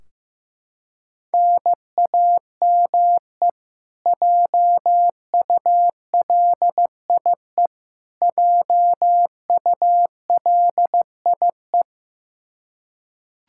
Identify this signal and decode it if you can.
Morse code, heard as a keyed tone, about 15 wpm, 700 Hz. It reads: NAME JULIE JULIE
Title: Practice QSO - Exchange 1 - Beginning - 15wpm